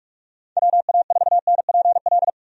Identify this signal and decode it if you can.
Morse code, heard as a keyed tone, about 46 wpm, 700 Hz. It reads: WA4NPL
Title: USA Call Signs - 6 Characters - Repeated 3x - 45wpm